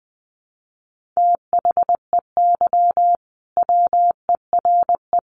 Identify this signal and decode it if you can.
Morse code, heard as a keyed tone, about 20 wpm, 700 Hz. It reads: THEY WERE
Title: Sentences from Top 1400 Words 20wpm